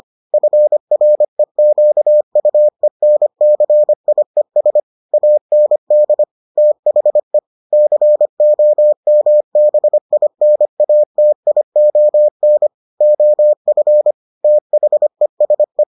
Morse code, heard as a keyed tone, about 25 wpm, 600 Hz. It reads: FREQUENCIES AND THE COMBINATION OF THESE